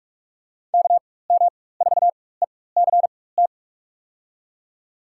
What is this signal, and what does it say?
Morse code, heard as a keyed tone, about 46 wpm, spaced as for 21 wpm, 700 Hz. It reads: KM4ECT
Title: USA Call Signs - 6 Characters - 4x Character Spacing - 45wpm